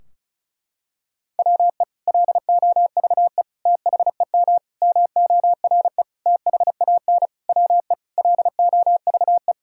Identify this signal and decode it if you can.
Morse code, heard as a keyed tone, about 36 wpm, 700 Hz. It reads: WE LOVE THEM MORE THAN WE LOVE
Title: Sentences from Top 1600 Words 35wpm